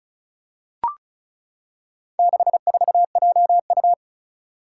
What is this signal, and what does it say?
Morse code, heard as a keyed tone, about 34 wpm, 700 Hz. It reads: 64JU
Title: Four Challenging Letter-Number Combinations 35wpm